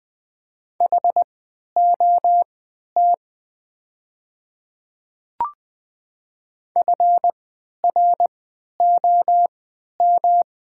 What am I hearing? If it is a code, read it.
Morse code, heard as a keyed tone, about 20 wpm, spaced as for 11 wpm, 700 Hz. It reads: HOT FROM
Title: 3x Character Spacing - Top 100 Words - 20wpm